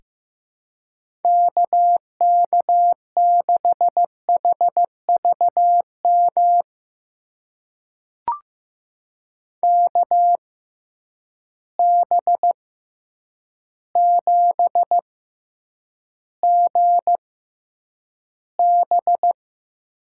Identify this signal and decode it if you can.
Morse code, heard as a keyed tone, about 15 wpm, spaced as for 8 wpm, 700 Hz. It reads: KK6HVM KB7GB
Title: USA Call Signs - 6 Characters - 6x Character Spacing - 15wpm